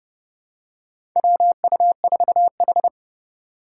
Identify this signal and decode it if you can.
Morse code, heard as a keyed tone, about 30 wpm, 700 Hz. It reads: WU4H
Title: USA Call Signs - 4 Characters - 6x Character Spacing - 30wpm